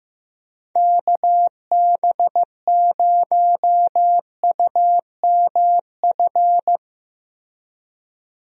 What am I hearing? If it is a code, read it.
Morse code, heard as a keyed tone, about 15 wpm, 700 Hz. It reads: KB0UMF
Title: USA Call Signs - 6 Characters - 15wpm